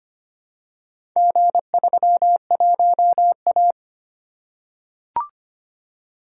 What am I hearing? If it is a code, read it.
Morse code, heard as a keyed tone, about 25 wpm, 700 Hz. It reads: G31A
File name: practice / 25wpm / Four Letter-Number Combinations - 25wpm